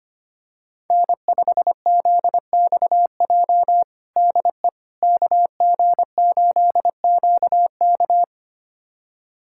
Morse code, heard as a keyed tone, about 25 wpm, 700 Hz. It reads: N5ZXJ DE KG8QK
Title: USA Callsign DE Callsign - 5 Characters - 25wpm